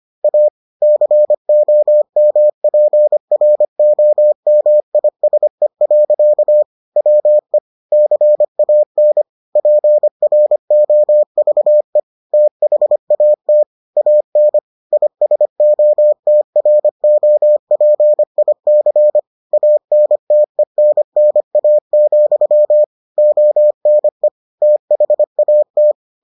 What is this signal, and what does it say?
Morse code, heard as a keyed tone, about 25 wpm, 600 Hz. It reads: A COMPROMISE. WE CAN PROVE THAT AN ISOTROPIC ANTENNA, ONE THAT